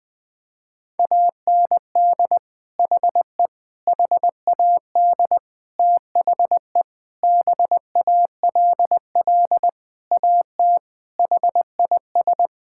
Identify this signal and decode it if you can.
Morse code, heard as a keyed tone, about 20 wpm, 700 Hz. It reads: AND HE HAD THE BALL AT HIS
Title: Sentences from Top 1000 Words - Review Part 6 - 20wpm